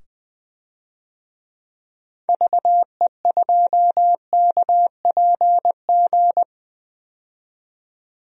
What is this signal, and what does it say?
Morse code, heard as a keyed tone, about 20 wpm, 700 Hz. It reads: VE2KPG
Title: DXCC-100 Call Signs - 6 Characters - 20wpm